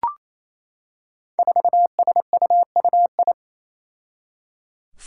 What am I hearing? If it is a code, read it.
Morse code, heard as a keyed tone, about 28 wpm, 700 Hz. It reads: 4SUUI
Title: Five Challenging Letter-Number Combinations 28wpm